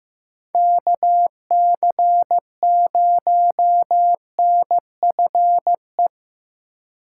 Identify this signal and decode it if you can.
Morse code, heard as a keyed tone, about 15 wpm, 700 Hz. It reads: KC0NFE
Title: USA Call Signs - 6 Characters - 2x Character Spacing - 15wpm